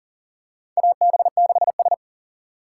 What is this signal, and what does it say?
Morse code, heard as a keyed tone, about 40 wpm, 690 Hz. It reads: AB6S
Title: USA Call Signs - 2x Character Spacing - 40wpm